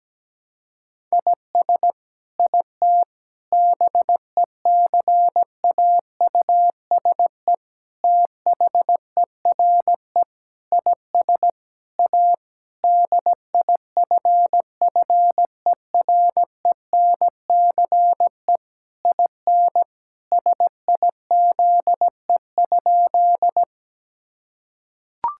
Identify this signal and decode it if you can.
Morse code, heard as a keyed tone, about 17 wpm, 700 Hz. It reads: IS IT BECAUSE THERE IS A DIFFERENCE IN SIZE?
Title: Sentences from Top 1000 Words - Review Part 5 - 17wpm